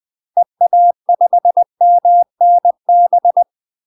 Morse code, heard as a keyed tone, about 20 wpm, 700 Hz. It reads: EA5MNB
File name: UK MASTER QSOs TRAINING / 20WPM / UK MASTER QSOs 0023